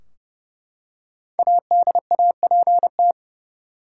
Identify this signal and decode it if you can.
Morse code, heard as a keyed tone, about 29 wpm, 700 Hz. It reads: ADAPT
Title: Five Letter Words - 30wpm